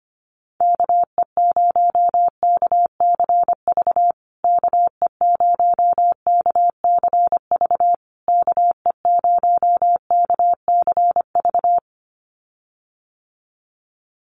Morse code, heard as a keyed tone, about 25 wpm, 700 Hz. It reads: KE0KCV KE0KCV KE0KCV
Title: USA Call Signs Repeated 3x 25wpm